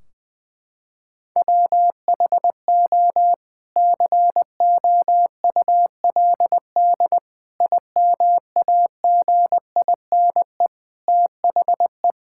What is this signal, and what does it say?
Morse code, heard as a keyed tone, about 20 wpm, 700 Hz. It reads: WHO COULD IMAGINE THE
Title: Sentences from Top 1600 Words 20wpm